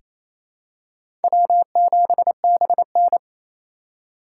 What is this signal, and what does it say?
Morse code, heard as a keyed tone, about 28 wpm, 700 Hz. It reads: W7BN
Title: USA Call Signs - 4 Characters - 2x Character Spacing - 28wpm